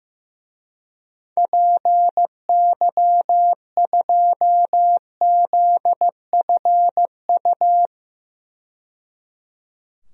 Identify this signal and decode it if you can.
Morse code, heard as a keyed tone, about 15 wpm, 700 Hz. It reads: PY2ZFU
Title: DXCC-100 Call Signs - 6 Characters - 15wpm